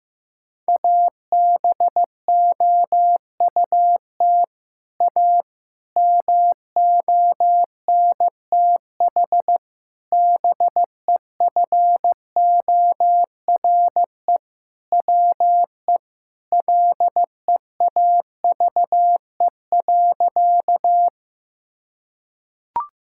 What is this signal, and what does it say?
Morse code, heard as a keyed tone, about 16 wpm, 700 Hz. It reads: ABOUT A MONTH BEFORE WE LEAVE.
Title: Sentences from Top 1000 Words - Review Part 5 - 15wpm